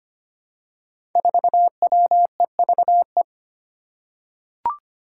Morse code, heard as a keyed tone, about 25 wpm, 700 Hz. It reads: 4WEVE
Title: Five Challenging Letter-Number Combinations 25wpm